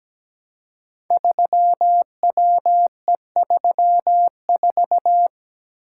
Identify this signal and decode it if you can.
Morse code, heard as a keyed tone, about 17 wpm, 700 Hz. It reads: 3WE34
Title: Code Groups - 5 Letter:Numbers - 17wpm